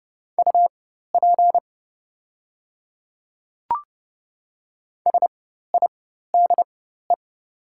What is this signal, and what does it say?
Morse code, heard as a keyed tone, about 30 wpm, spaced as for 14 wpm, 700 Hz. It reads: UP SIDE
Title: 4x Character Spacing - Top 100 Words - 30wpm